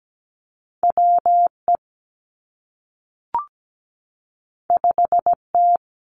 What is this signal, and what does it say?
Morse code, heard as a keyed tone, about 17 wpm, 700 Hz. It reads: WE 5T
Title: Two Challenging Letter-Number Combinations 17wpm